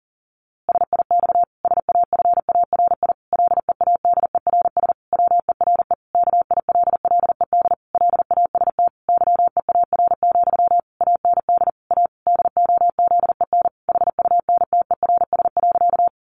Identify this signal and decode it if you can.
Morse code, heard as a keyed tone, about 40 wpm, 700 Hz. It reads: SIX SAFARI LEADERS WERE KILLED LAST YEAR, AND A DOZEN HUNTERS.
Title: A Sound Of Thunder 40wpm